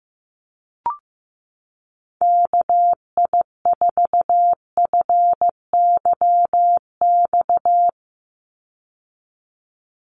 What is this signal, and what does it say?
Morse code, heard as a keyed tone, about 15 wpm, 700 Hz. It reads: KI4FYX
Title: USA Call Signs 15wpm